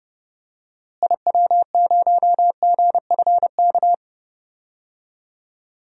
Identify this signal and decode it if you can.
Morse code, heard as a keyed tone, about 32 wpm, 700 Hz. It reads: IW0GFK
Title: DXCC-100 Call Signs - 6 Characters - Repeated 2x - 30wpm